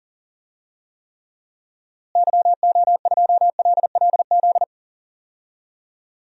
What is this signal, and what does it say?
Morse code, heard as a keyed tone, about 40 wpm, 700 Hz. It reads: YO2LLZ
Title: DXCC-100 Call Signs - 6 Characters - Repeated 2x - 40wpm